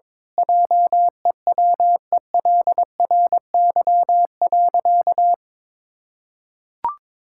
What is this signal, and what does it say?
Morse code, heard as a keyed tone, about 22 wpm, 700 Hz. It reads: JEWELRY.
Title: Pangram Challenge v2 - 22wpm